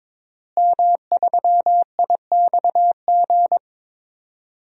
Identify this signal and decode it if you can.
Morse code, heard as a keyed tone, about 22 wpm, 700 Hz. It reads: M3IXG
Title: DXCC-100 Call Signs - 5 Characters - Repeated 3x - 22wpm